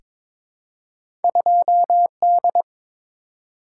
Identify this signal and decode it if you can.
Morse code, heard as a keyed tone, about 22 wpm, 700 Hz. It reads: 2D
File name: practice / 22wpm / Two Letter-Number Combinations - 22wpm